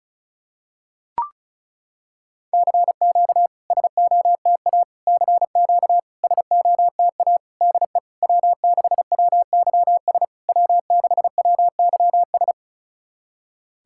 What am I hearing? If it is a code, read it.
Morse code, heard as a keyed tone, about 36 wpm, 700 Hz. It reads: CQ SOTA CQ SOTA DE W6WYS W6WYS
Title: SOTA QSO Practice 35wpm